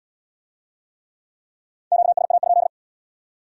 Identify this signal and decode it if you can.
Morse code, heard as a keyed tone, about 64 wpm, 700 Hz. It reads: QSA?